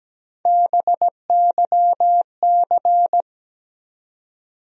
Morse code, heard as a keyed tone, about 17 wpm, 700 Hz. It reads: BYC
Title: USA Call Sign Postfixes - 3 Letters 17wpm